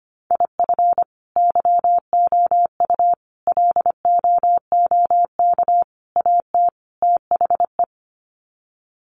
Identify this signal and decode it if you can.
Morse code, heard as a keyed tone, about 25 wpm, 700 Hz. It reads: IF YOU LOOK AT THE
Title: Sets of 5 Words 25wpm